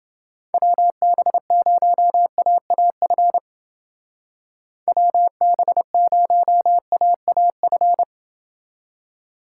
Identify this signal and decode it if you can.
Morse code, heard as a keyed tone, about 29 wpm, 700 Hz. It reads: WB0AAF WB0AAF